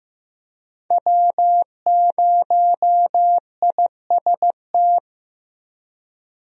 Morse code, heard as a keyed tone, about 15 wpm, 700 Hz. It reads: W0IST
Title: USA Call Signs - 5 Characters - 15wpm